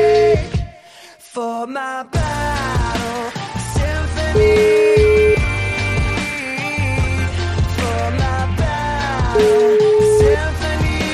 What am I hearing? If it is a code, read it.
Ringback tone (Europe, 425 Hz), with noise.